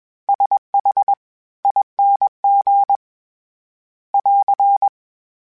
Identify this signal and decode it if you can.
Morse code, heard as a keyed tone, about 21 wpm, spaced as for 18 wpm, 800 Hz. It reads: SHING <AR>
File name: Special INGs-mp3s / Special INGs-mp3s / SlowToFaster ing @20